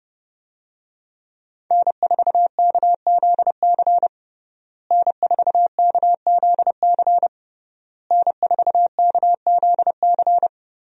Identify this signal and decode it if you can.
Morse code, heard as a keyed tone, about 29 wpm, 700 Hz. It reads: N4KZC N4KZC N4KZC
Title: USA Call Signs - 5 Characters - Repeated 3x - 30wpm